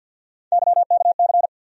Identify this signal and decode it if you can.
Morse code, heard as a keyed tone, about 51 wpm, 700 Hz. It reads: YKC
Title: USA Call Sign Postfixes - 3 Letters 50wpm